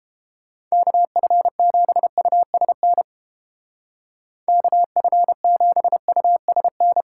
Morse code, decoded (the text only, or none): KF7USN KF7USN